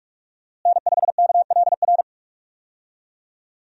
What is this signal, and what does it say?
Morse code, heard as a keyed tone, about 46 wpm, 700 Hz. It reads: N5KLR
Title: USA Call Signs - 5 Characters - Repeated 3x - 45wpm